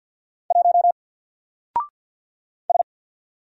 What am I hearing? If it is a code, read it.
Morse code, heard as a keyed tone, about 48 wpm, 700 Hz. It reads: 1 S
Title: Single Character (Letter:Number:Punctuation:Prosign) 50wpm